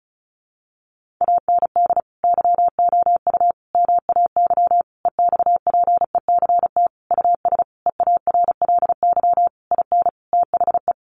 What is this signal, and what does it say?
Morse code, heard as a keyed tone, about 34 wpm, 700 Hz. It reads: AND YOU MAY EXPECT US EARLY IN THE
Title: Sentences from Top 1500 Words 35wpm